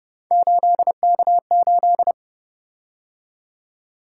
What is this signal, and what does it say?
Morse code, heard as a keyed tone, about 30 wpm, 700 Hz. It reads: 8K8
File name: Speed-Racing - Three Letter-Number Combinations - 25wpm